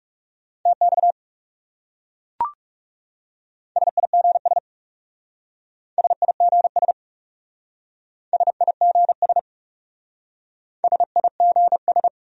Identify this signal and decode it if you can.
Morse code, heard as a keyed tone, about 37 wpm, 700 Hz. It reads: TX SIGS SIGS SIGS SIGS
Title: Speed-Racing Abbreviations - 30wpm